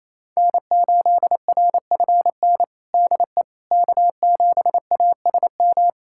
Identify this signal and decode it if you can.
Morse code, heard as a keyed tone, about 28 wpm, 700 Hz. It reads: N8RFN DE K7ASM
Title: USA Callsign DE Callsign - 5 Characters - 28wpm